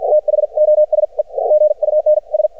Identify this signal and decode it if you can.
Morse code, heard as a keyed tone, about 53 wpm, spaced as for 43 wpm, 610 Hz. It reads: THOSE GUNS
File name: archived / CRSnet